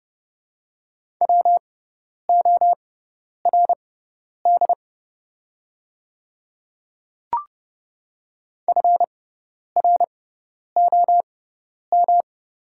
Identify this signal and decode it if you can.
Morse code, heard as a keyed tone, about 29 wpm, spaced as for 10 wpm, 700 Hz. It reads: WORD FROM
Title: 6x Character Spacing - Top 100 Words - 30wpm